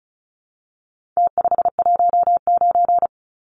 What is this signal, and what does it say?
Morse code, heard as a keyed tone, about 34 wpm, 700 Hz. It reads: T519
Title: Numbers - Contest Style - 4-Digit 35wpm